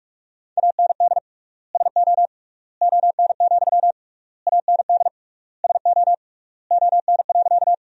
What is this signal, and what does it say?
Morse code, heard as a keyed tone, about 45 wpm, 700 Hz. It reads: AND SO ON, AND SO ON.
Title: 3x Word-Spacing - Sentences from Top 100 Words - 45wpm